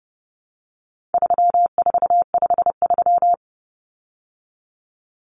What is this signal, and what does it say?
Morse code, heard as a keyed tone, about 30 wpm, 700 Hz. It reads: 3453
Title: Numbers - 4-Digit 30wpm